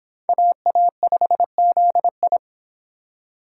Morse code, heard as a keyed tone, about 26 wpm, 700 Hz. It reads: AA5ZI